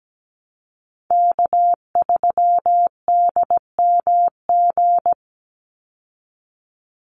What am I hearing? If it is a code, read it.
Morse code, heard as a keyed tone, about 17 wpm, 700 Hz. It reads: K3DMG
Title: Call Signs Repeated 2x - 17wpm